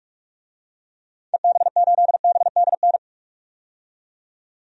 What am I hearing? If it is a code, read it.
Morse code, heard as a keyed tone, about 42 wpm, 700 Hz. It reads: EB8BDN
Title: DXCC-100 Call Signs - 6 Characters - 45wpm